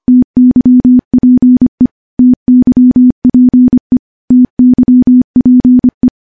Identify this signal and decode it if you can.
Morse code, heard as a keyed tone, about 25 wpm, 250 Hz. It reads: TYPE TYPE TYPE
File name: morse / morse-training-25WPM